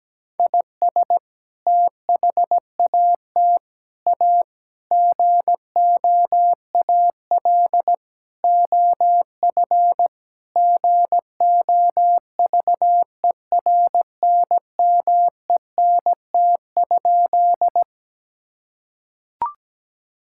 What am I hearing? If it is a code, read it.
Morse code, heard as a keyed tone, about 17 wpm, 700 Hz. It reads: IS THAT A GOAL OF GOVERNMENT?
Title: Sentences from Top 1000 Words - Review Part 5 - 17wpm